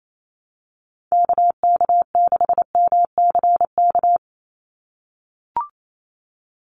Morse code, heard as a keyed tone, about 28 wpm, 700 Hz. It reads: KK6MCK
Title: USA Call Signs 28wpm